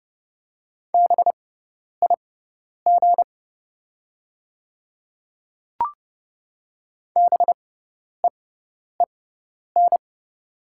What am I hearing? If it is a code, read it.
Morse code, heard as a keyed tone, about 30 wpm, spaced as for 10 wpm, 700 Hz. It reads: BIG BEEN
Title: 6x Character Spacing - Top 200 Words - 30wpm